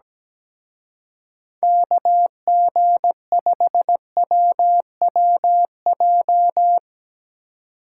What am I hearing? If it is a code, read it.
Morse code, heard as a keyed tone, about 18 wpm, 700 Hz. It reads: KG5WWJ
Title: Speed Racing - USA 2x3 Call Signs 17wpm